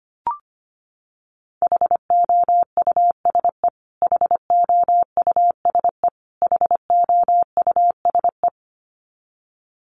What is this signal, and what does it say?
Morse code, heard as a keyed tone, about 25 wpm, 700 Hz. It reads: HOUSE HOUSE HOUSE
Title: Top 500 Words Repeated 25wpm